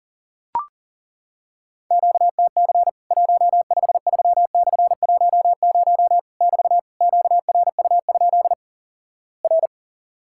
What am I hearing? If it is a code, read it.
Morse code, heard as a keyed tone, about 39 wpm, 700 Hz. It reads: QTC 153/10 <BT> QRU? R
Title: Worked All Europe DX Contest - QTC Exchange - 40wpm